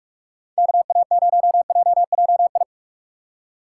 Morse code, heard as a keyed tone, about 44 wpm, 700 Hz. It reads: KA0JJI